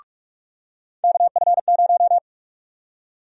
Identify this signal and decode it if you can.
Morse code, heard as a keyed tone, about 43 wpm, 700 Hz. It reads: KF0
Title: USA Call Sign Prefixes - 45wpm